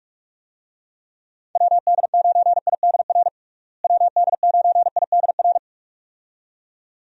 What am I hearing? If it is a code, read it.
Morse code, heard as a keyed tone, about 46 wpm, 700 Hz. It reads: WD9IDR WD9IDR
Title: USA Call Signs - 6 Characters - Repeated 2x - 45wpm